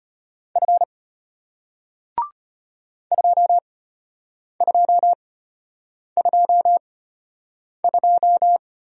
Morse code, heard as a keyed tone, about 34 wpm, spaced as for 8 wpm, 700 Hz. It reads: F 2222